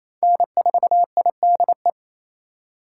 Morse code, heard as a keyed tone, about 28 wpm, 700 Hz. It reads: N4IDE
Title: USA Call Signs - 5 Characters - 4x Character Spacing - 28wpm